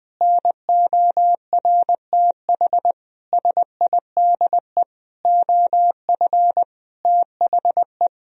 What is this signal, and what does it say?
Morse code, heard as a keyed tone, about 20 wpm, 700 Hz. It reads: NORTH SIDE OF THE
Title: Sentences from Top 1000 Words - Review Part 1 - 20wpm